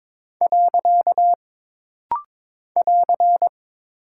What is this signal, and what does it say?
Morse code, heard as a keyed tone, about 22 wpm, 700 Hz. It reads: . <AR>
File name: practice / 22wpm / Single Punctuation-Prosign - Rapid-Fire - 22wpm